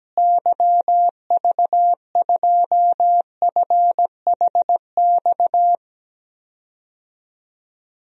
Morse code, heard as a keyed tone, about 17 wpm, 700 Hz. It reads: YV2FHX
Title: DXCC-100 Call Signs - 6 Characters - Repeated 2x - 17wpm